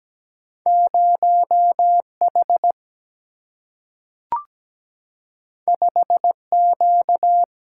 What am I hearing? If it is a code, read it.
Morse code, heard as a keyed tone, about 17 wpm, 700 Hz. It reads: 0H 5Q
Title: Single-Letter Single-Number 17wpm